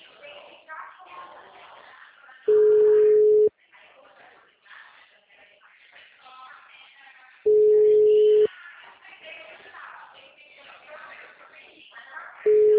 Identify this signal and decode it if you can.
Ringback tone (Europe, 425 Hz), with speech.